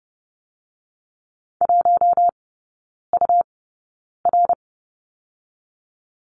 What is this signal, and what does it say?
Morse code, heard as a keyed tone, about 30 wpm, spaced as for 9 wpm, 700 Hz. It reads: 1UR
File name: Lesson 17e - Letter u - Three Characters 30wpm